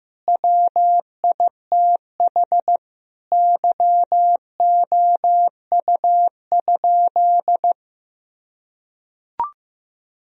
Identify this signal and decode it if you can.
Morse code, heard as a keyed tone, about 15 wpm, 700 Hz. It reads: WITH YOU?
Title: Sentences from Top 1000 Words - Review Part 1 - 15wpm